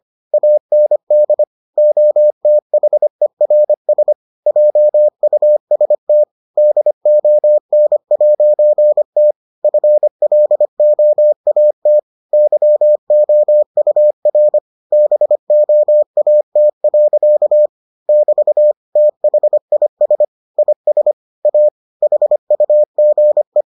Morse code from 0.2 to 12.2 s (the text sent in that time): AND OTHERS JUST DON'T FLOAT